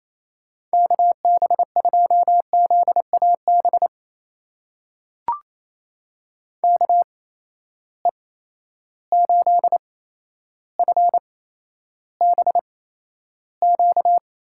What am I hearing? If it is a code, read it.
Morse code, heard as a keyed tone, about 28 wpm, spaced as for 12 wpm, 700 Hz. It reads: KB2ZAB KE8FBQ
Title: USA Call Signs - 6 Characters - 8x Character Spacing - 28wpm